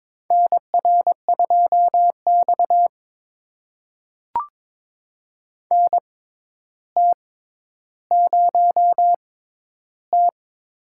Morse code, heard as a keyed tone, about 22 wpm, spaced as for 11 wpm, 700 Hz. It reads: NR2X NT0T
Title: USA Call Signs - 4 Characters - 6x Character Spacing - 22wpm